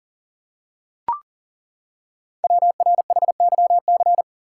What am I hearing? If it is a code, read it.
Morse code, heard as a keyed tone, about 39 wpm, 700 Hz. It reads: WRHYC
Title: Code Groups - 5 Letters - 40wpm